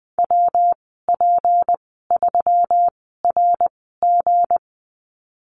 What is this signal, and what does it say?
Morse code, heard as a keyed tone, about 19 wpm, spaced as for 14 wpm, 700 Hz. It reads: WP3RG
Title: 200 Lesson 32g - Letter k - Call Signs 20wpm